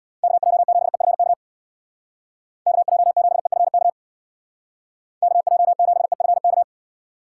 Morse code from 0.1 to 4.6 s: KJ7EFK KJ7EFK

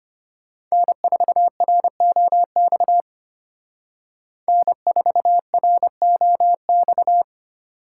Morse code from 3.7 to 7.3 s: N4ROX

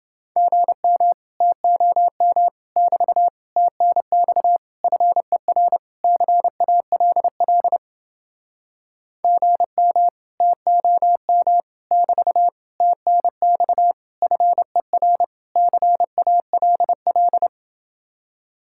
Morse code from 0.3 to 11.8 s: GM TOM <BT> TNX FER CALL GM TOM

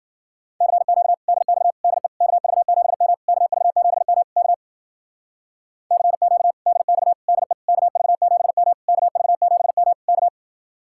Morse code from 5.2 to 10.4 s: CQ DX DE KV7M KV7M K

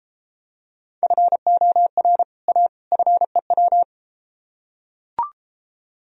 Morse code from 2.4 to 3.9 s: A FEW

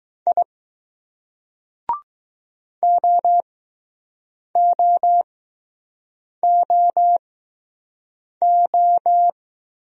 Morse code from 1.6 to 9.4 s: OOOO

